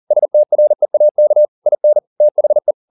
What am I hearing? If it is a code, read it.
Morse code, heard as a keyed tone, about 40 wpm, 600 Hz. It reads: STREAK IN THE